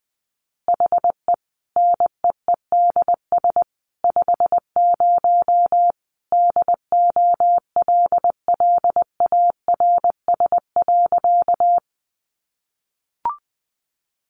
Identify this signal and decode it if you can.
Morse code, heard as a keyed tone, about 20 wpm, 700 Hz. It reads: HE NEEDS 50 DOLLARS.
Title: Additional Easy Phrases 20wpm